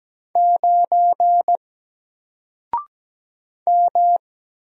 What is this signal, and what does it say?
Morse code, heard as a keyed tone, about 17 wpm, 700 Hz. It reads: 9 M
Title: Single Letter-Number Alt 17wpm